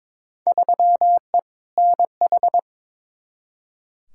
Morse code, heard as a keyed tone, about 21 wpm, 700 Hz. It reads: 3E NH
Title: Field Day Exchanges Encore - 22wpm